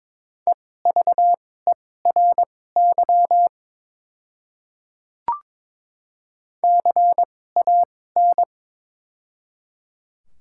Morse code, heard as a keyed tone, about 22 wpm, spaced as for 16 wpm, 700 Hz. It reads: EVERY CAN
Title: 2x Character Spacing - Top 200 Words - 22wpm